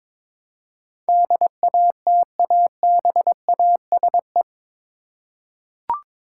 Morse code, heard as a keyed tone, about 21 wpm, 700 Hz. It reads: DATABASE
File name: practice / 22wpm / Eight Letter Words - 22wpm